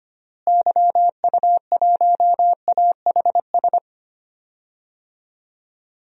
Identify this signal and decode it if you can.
Morse code, heard as a keyed tone, about 25 wpm, 700 Hz. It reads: YU1AHS
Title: DXCC-100 Call Signs - 6 Characters - Repeated 3x - 25wpm